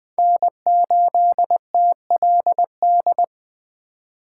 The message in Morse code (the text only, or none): N8TLD